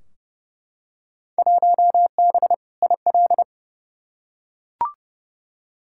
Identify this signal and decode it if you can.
Morse code, heard as a keyed tone, about 32 wpm, 700 Hz. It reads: 1B IL
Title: Field Day Exchanges - 30wpm